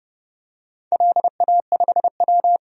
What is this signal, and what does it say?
Morse code, heard as a keyed tone, about 30 wpm, 700 Hz. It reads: LA5W
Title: DXCC-100 Call Signs - 4 Characters - 30wpm